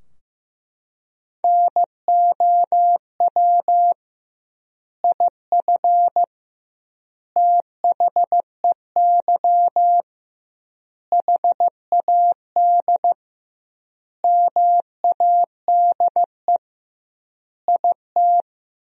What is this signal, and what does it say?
Morse code, heard as a keyed tone, about 15 wpm, 700 Hz. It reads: NOW IF THEY HAD MADE IT
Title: 2x Word-Spacing - Sentences from Top 500 Words - 15wpm